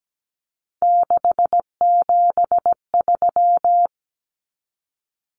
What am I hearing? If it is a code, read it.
Morse code, heard as a keyed tone, about 17 wpm, 700 Hz. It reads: 673
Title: Common Cut Numbers - 3-Digit 17wpm